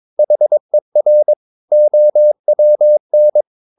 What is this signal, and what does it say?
Morse code, heard as a keyed tone, about 22 wpm, 600 Hz. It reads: HER OWN
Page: Romanen Never af Ken Follet på engelsk.